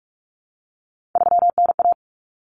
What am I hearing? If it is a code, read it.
Morse code, heard as a keyed tone, about 44 wpm, 700 Hz. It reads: 3NA
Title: Common Cut Numbers - 3-Digit 45wpm